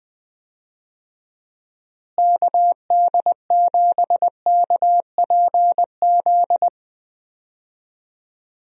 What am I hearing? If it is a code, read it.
Morse code, heard as a keyed tone, about 19 wpm, 700 Hz. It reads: KD7KPZ
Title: USA Call Signs - 6 Characters - Repeated 2x - 20wpm